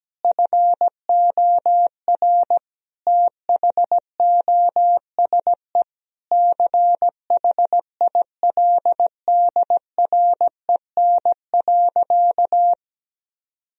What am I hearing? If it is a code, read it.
Morse code, heard as a keyed tone, about 17 wpm, 700 Hz. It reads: FOR THOSE CHILDREN.
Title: Sentences from Top 1000 Words - Review Part 1 - 17wpm